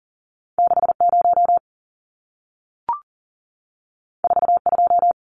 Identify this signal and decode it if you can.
Morse code, heard as a keyed tone, about 40 wpm, 700 Hz. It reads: 60 42
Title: Numbers - 2-Digit 40wpm